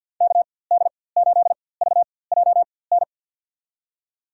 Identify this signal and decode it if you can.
Morse code, heard as a keyed tone, about 52 wpm, spaced as for 24 wpm, 700 Hz. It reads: KD8VJN
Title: USA Call Signs - 6 Characters - 4x Character Spacing - 50wpm